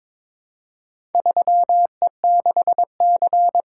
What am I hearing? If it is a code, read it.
Morse code, heard as a keyed tone, about 22 wpm, 700 Hz. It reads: 3E6C
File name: Four Letter-Number Combinations - 22wpm